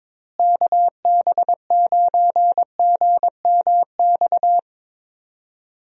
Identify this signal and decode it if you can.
Morse code, heard as a keyed tone, about 22 wpm, 700 Hz. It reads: KB9GMX
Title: USA Call Signs - 6 Characters - Repeated 3x - 22wpm